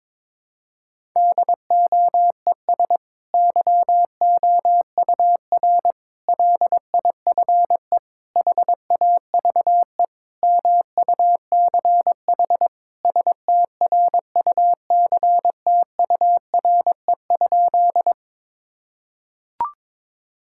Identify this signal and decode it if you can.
Morse code, heard as a keyed tone, about 22 wpm, 700 Hz. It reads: DOES YOUR LIFE HAVE MUCH STRUCTURE?
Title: Sentences from Top 1000 Words - Review Part 6 - 22wpm